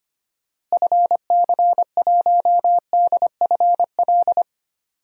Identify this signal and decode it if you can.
Morse code, heard as a keyed tone, about 25 wpm, 700 Hz. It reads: FC1DFL
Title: DXCC-100 Call Signs - 6 Characters - Repeated 2x - 25wpm